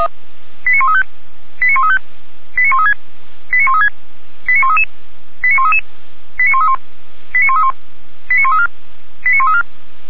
Five-tone selective call (ZVEI), heard as quick runs of five tones, tones of about 70 ms. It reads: 89135 89135 89136 89136 89133 89133 89131 89131 89134 89134